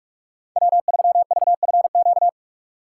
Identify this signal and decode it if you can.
Morse code, heard as a keyed tone, about 46 wpm, 700 Hz. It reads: W3VFQ
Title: USA Call Signs - 5 Characters - 3x Character Spacing - 45wpm